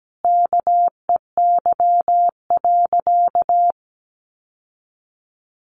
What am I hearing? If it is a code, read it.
Morse code, heard as a keyed tone, about 17 wpm, 700 Hz. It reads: KEY.
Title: Sentences from Top 600 Words 17wpm